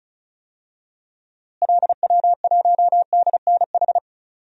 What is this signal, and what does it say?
Morse code, heard as a keyed tone, about 34 wpm, 700 Hz. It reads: LW1DNH
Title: DXCC-100 Call Signs - 6 Characters - Repeated 2x - 35wpm